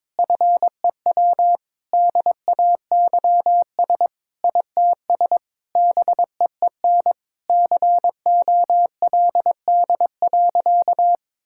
Morse code, heard as a keyed tone, about 22 wpm, 700 Hz. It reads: FEW DAYS ITS BEEN COLD.
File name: practice / 22wpm / Sentences with 3-Word Phrases - 22wpm